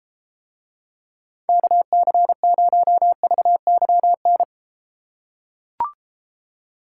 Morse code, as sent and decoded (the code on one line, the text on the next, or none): -.- -.-. ----- ...- -.-- -.
KC0VYN